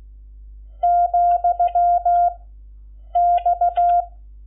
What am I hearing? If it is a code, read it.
Morse code, heard as a keyed tone, about 16 wpm, 700 Hz. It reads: , X
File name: Lecciones / CLASE Nº11